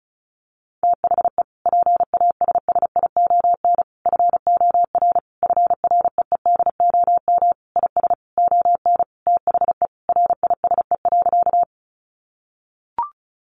Morse code, heard as a keyed tone, about 34 wpm, 700 Hz. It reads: THE PASSION FOR FREEDOM IS ON THE RISE.
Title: Sentences from Top 1300 Words 35wpm